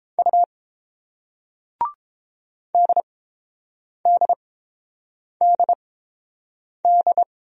Morse code, heard as a keyed tone, about 30 wpm, spaced as for 7 wpm, 700 Hz. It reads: U DDDD